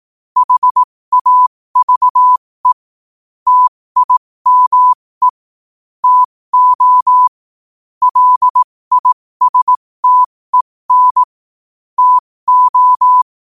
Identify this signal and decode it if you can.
Morse code, heard as a keyed tone, about 18 wpm, spaced as for 15 wpm, 1000 Hz. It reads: HAVE TIME TO LISTEN TO
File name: qotdfiles / 15wpm